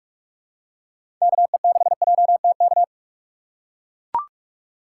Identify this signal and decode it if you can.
Morse code, heard as a keyed tone, about 43 wpm, 700 Hz. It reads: KE6JTK
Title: USA Call Signs - 2x Character Spacing - 45wpm